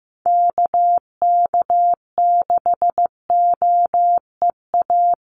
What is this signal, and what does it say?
Morse code, heard as a keyed tone, about 15 wpm, 700 Hz. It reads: KK6OEA
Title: USA Call Signs Repeated 2x 15wpm